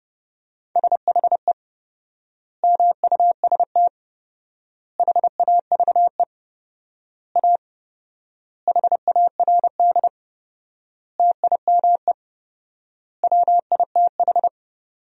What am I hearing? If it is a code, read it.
Morse code, heard as a keyed tone, about 30 wpm, 700 Hz. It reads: SHE MUST HAVE A HARD TIME WITH
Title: 4x Word-Spacing - Sentences from Top 300 Words - 30wpm